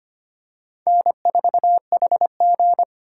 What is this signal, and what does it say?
Morse code, heard as a keyed tone, about 25 wpm, 700 Hz. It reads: N4HG
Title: USA Call Signs - 4 Characters - 2x Character Spacing - 25wpm